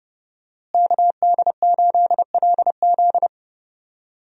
Morse code, heard as a keyed tone, about 32 wpm, 700 Hz. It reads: KD8LZ
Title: USA Call Signs - 5 Characters - Repeated 3x - 30wpm